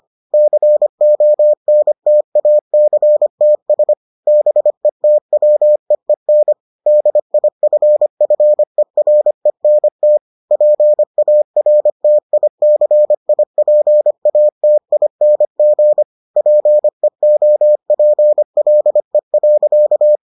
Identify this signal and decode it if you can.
Morse code, heard as a keyed tone, about 25 wpm, 600 Hz. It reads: CONTACTS BETWEEN DIFFERENT PARTICIPATING PEOPLE.